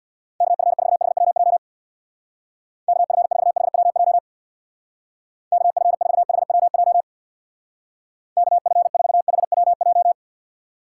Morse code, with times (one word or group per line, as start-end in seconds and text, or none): 0.4-1.6 KF4HWJ
2.9-4.2 KF4HWJ
5.5-7.0 KF4HWJ
8.4-10.1 KF4HWJ